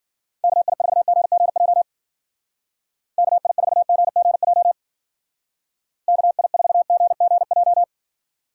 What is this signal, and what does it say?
Morse code, heard as a keyed tone, about 51 wpm, 700 Hz. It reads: KI4GGJ KI4GGJ KI4GGJ